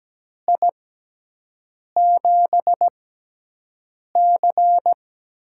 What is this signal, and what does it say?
Morse code, heard as a keyed tone, about 17 wpm, spaced as for 6 wpm, 700 Hz. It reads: I7C
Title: USA Call Signs - 6 Characters - 6x Character Spacing - 17wpm